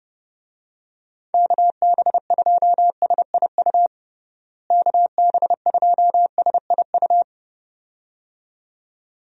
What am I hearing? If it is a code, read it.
Morse code, heard as a keyed tone, about 30 wpm, 700 Hz. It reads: KB2SIU KB2SIU
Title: USA Call Signs - 6 Characters - Repeated 2x - 30wpm